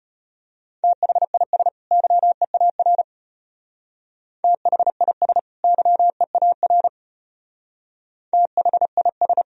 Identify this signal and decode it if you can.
Morse code, heard as a keyed tone, about 30 wpm, 700 Hz. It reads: THIS YEAR THIS YEAR THIS
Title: Speed-Racing - Sets of 2 Words 25wpm